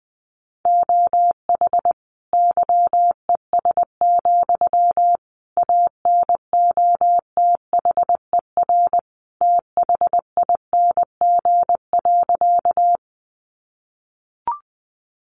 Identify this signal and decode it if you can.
Morse code, heard as a keyed tone, about 20 wpm, 700 Hz. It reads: OH YES, ANOTHER THING.
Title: Sentences from Top 600 Words 20wpm